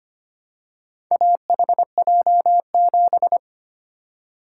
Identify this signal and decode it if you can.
Morse code, heard as a keyed tone, about 25 wpm, 700 Hz. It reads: AHJ7
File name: practice / 25wpm / Four Letter-Number Combinations - 25wpm